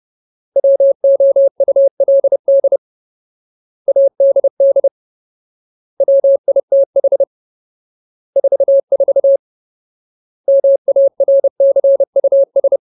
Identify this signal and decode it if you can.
Morse code, heard as a keyed tone, about 30 wpm, 550 Hz. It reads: WOULD ADD WITH 44 MARCUS
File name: Ultimate Word List / 30WPM Ultimate